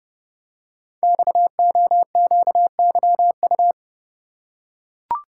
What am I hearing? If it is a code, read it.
Morse code, heard as a keyed tone, about 30 wpm, 700 Hz. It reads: XOQYU
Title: Code Groups - 5 Letters - 30wpm